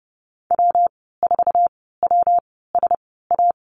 Morse code, heard as a keyed tone, about 30 wpm, spaced as for 17 wpm, 700 Hz. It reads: W4WSA
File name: Lesson 19g - Letter w - Call Signs 30wpm